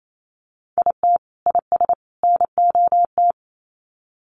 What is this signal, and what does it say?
Morse code, heard as a keyed tone, about 28 wpm, 700 Hz. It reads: IT IS NOT
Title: Sets of 3 Words 28wpm